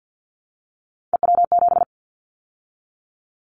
Morse code, heard as a keyed tone, about 50 wpm, 700 Hz. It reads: EW7
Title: Callsign Prefixes for Top 100 DXCC - 50wpm